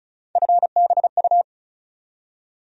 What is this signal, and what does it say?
Morse code, heard as a keyed tone, about 36 wpm, 700 Hz. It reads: FBU
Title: USA Call Sign Postfixes - 3 Letters 35wpm